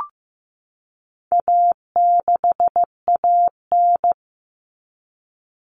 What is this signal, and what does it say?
Morse code, heard as a keyed tone, about 15 wpm, 700 Hz. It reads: A6AN
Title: Common Cut Numbers - 4-Digit 15wpm